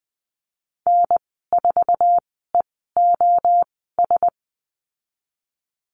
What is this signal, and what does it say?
Morse code, heard as a keyed tone, about 20 wpm, spaced as for 14 wpm, 700 Hz. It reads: N4EOS
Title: 055 Lesson 10g - Letter r - Call Signs 20wpm